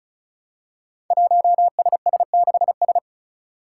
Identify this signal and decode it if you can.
Morse code, heard as a keyed tone, about 36 wpm, 700 Hz. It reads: 1SS6S
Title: Five Challenging Letter-Number Combinations 35wpm